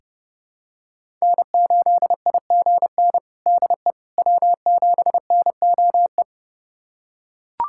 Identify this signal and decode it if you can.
Morse code, heard as a keyed tone, about 30 wpm, 700 Hz. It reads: N8IGN DE W7NOE
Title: USA Callsign DE Callsign - 5 Characters - 30wpm